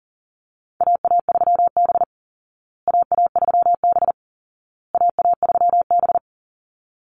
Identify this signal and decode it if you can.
Morse code, heard as a keyed tone, about 40 wpm, 700 Hz. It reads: AA3B AA3B AA3B
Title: Call Signs Repeated 2x - 40wpm